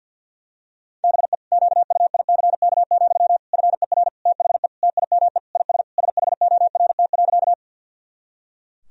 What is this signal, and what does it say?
Morse code, heard as a keyed tone, about 52 wpm, 700 Hz. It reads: BE QUICK, FER THE TIME IS SHORT.
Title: Sentences from Top 500 Words and Abbreviations 50wpm